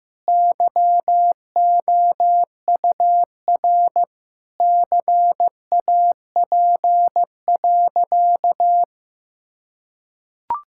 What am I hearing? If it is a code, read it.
Morse code, heard as a keyed tone, about 16 wpm, 700 Hz. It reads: YOUR CAP.
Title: Sentences from Top 1600 Words 15wpm